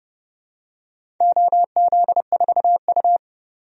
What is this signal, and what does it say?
Morse code, heard as a keyed tone, about 31 wpm, 700 Hz. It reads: OZ4U
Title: DXCC-100 Call Signs - 4 Characters - 30wpm